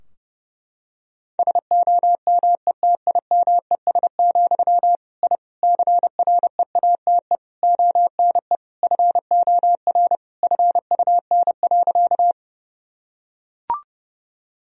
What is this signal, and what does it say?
Morse code, heard as a keyed tone, about 30 wpm, 700 Hz. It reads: SOMETIMES, I CREATE ONE FOR FUN.
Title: Sentences from Top 1000 Words - Review Part 3 - 30wpm